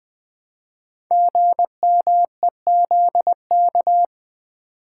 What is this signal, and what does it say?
Morse code, heard as a keyed tone, about 20 wpm, 700 Hz. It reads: GMEZK
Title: Code Groups - 5 Letter:Numbers - 20wpm